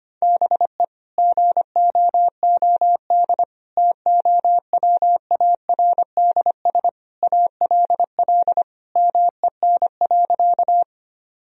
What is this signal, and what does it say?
Morse code, heard as a keyed tone, about 25 wpm, 700 Hz. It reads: BE GOOD TOWARDS ALL MEN.
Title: Sentences from Top 1000 Words - Review Part 2 - 25wpm